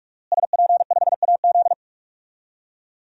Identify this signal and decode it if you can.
Morse code, heard as a keyed tone, about 46 wpm, 700 Hz. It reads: SP5AZ
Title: DXCC-100 Call Signs - 5 Characters - Repeated 3x - 45wpm